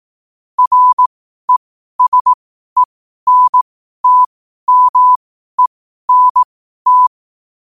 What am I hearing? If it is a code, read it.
Morse code, heard as a keyed tone, about 18 wpm, spaced as for 12 wpm, 1000 Hz. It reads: RESENTMENT
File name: qotdfiles / 12wpm